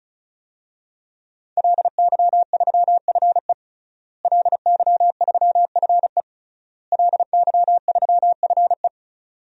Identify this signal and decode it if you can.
Morse code, heard as a keyed tone, about 36 wpm, 700 Hz. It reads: LY3FE LY3FE LY3FE
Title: DXCC-100 Call Signs - 5 Characters - Repeated 3x - 35wpm